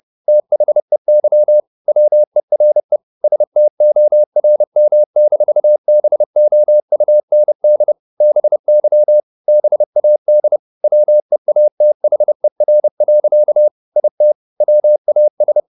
Morse code, heard as a keyed tone, about 30 wpm, 600 Hz. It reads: THEY WERE STORM-BOUND BY BAD WEATHER. IT WAS
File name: the old man and the sea / the old man and the sea 1 30wpm